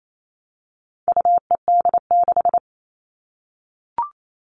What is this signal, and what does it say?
Morse code, heard as a keyed tone, about 28 wpm, 700 Hz. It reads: UED6
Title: All Cut Numbers - 4-Digit 28wpm